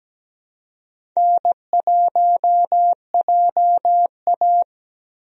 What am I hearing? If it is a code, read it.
Morse code, heard as a keyed tone, about 17 wpm, 700 Hz. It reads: N1JA
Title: USA Call Signs - 4 Characters - 17wpm